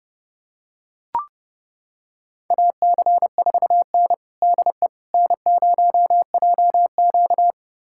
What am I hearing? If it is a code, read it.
Morse code, heard as a keyed tone, about 30 wpm, 700 Hz. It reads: AC4N DE N0JQ
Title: USA Callsign DE Callsign - 4 Characters - 30wpm